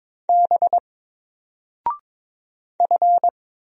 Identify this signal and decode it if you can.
Morse code, heard as a keyed tone, about 22 wpm, 700 Hz. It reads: B F
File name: Single-Letter-Number-22wpm